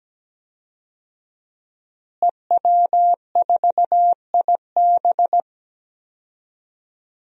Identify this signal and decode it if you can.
Morse code, heard as a keyed tone, about 17 wpm, 700 Hz. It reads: EW4IB
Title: DXCC-100 Call Signs - 5 Characters - Repeated 3x - 17wpm